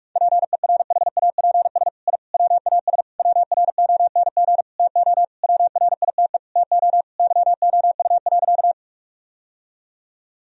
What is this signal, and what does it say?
Morse code, heard as a keyed tone, about 46 wpm, 700 Hz. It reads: PERHAPS I WAS WRONG TO WRITE TO YOU.
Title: Sentences from Top 1000 Words - Review Part 1 - 45wpm